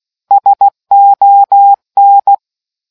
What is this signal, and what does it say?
Morse code, heard as a keyed tone, about 16 wpm, 800 Hz. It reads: SON